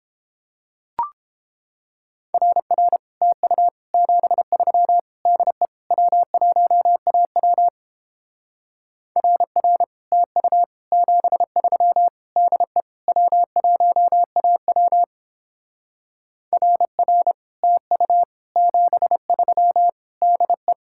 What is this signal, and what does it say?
Morse code, heard as a keyed tone, about 31 wpm, 700 Hz. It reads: RR TU 73 DE W1AW RR TU 73 DE W1AW RR TU 73 DE